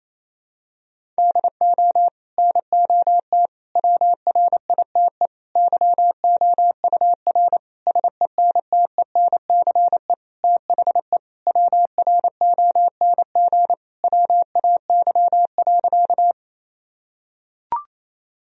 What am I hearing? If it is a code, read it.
Morse code, heard as a keyed tone, about 28 wpm, 700 Hz. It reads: DO NOT WRITE YOUR SENTENCE THE WRONG WAY.
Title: Sentences from Top 1000 Words - Review Part 1 - 28wpm